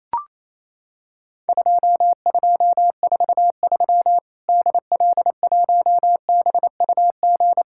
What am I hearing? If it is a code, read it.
Morse code, heard as a keyed tone, about 27 wpm, 700 Hz. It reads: 2243 DL1BUG
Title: Worked All Europe DX Contest - QTC Exchange - 28wpm